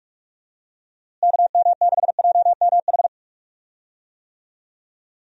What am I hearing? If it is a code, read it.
Morse code, heard as a keyed tone, about 41 wpm, 700 Hz. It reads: KM6JMH
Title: USA Call Signs - 6 Characters - Repeated 3x - 45wpm